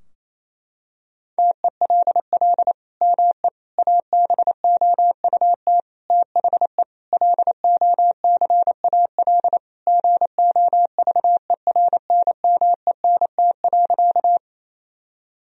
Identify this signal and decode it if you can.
Morse code, heard as a keyed tone, about 27 wpm, 700 Hz. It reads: TELL ME ABOUT THE LOCAL GOVERNMENT.
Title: Sentences from Top 1000 Words - Review Part 3 - 28wpm